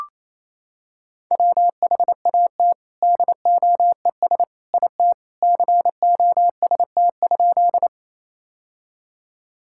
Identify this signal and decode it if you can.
Morse code, heard as a keyed tone, about 28 wpm, 700 Hz. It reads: WHAT DOES IT COST?
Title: Sentences from Top 1000 Words - Review Part 5 - 28wpm